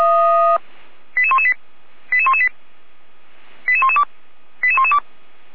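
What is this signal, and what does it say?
Five-tone selective call (ZVEI), heard as quick runs of five tones, tones of about 70 ms. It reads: 88118 88118 88112 88112
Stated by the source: ZVEI five-tone selective call 88491 received by SDR.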